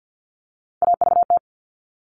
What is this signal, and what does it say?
Morse code, heard as a keyed tone, about 50 wpm, 700 Hz. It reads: AVT
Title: All Cut Numbers - 3-Digit 50wpm